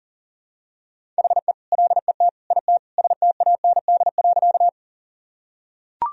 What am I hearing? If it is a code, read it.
Morse code, heard as a keyed tone, about 40 wpm, 700 Hz. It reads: HE LET IT STAND.
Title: Sentences from Top 1000 Words - Review Part 1 - 40wpm